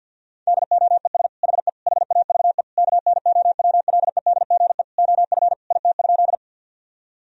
Morse code, heard as a keyed tone, about 49 wpm, 700 Hz. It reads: DOES HE HAVE KNOWLEDGE OF IT?
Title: Sentences from Top 1000 Words - Review Part 3 - 50wpm